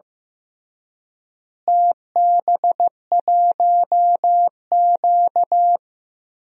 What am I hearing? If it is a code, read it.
Morse code, heard as a keyed tone, about 15 wpm, 700 Hz. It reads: TB1Q
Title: DXCC-100 Call Signs - 4 Characters - Repeated 3x - 15wpm